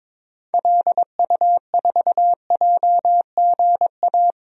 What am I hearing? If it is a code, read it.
Morse code, heard as a keyed tone, about 22 wpm, 700 Hz. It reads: LU4JGA
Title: DXCC-100 Call Signs - 6 Characters - Repeated 3x - 22wpm